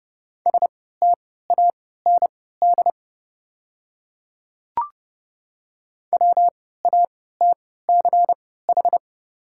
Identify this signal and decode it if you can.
Morse code, heard as a keyed tone, about 30 wpm, spaced as for 17 wpm, 700 Hz. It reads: STAND WATCH
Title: 3x Character Spacing - Top 300 Words - 30wpm